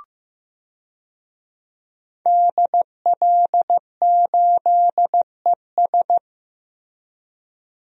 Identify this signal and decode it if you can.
Morse code, heard as a keyed tone, about 15 wpm, 700 Hz. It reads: DL8ES
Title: DXCC-100 Call Signs - 5 Characters - Repeated 3x - 15wpm